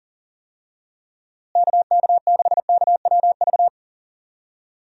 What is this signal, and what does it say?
Morse code, heard as a keyed tone, about 40 wpm, 700 Hz. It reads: KK6KWV
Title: USA Call Signs - 6 Characters - Repeated 3x - 40wpm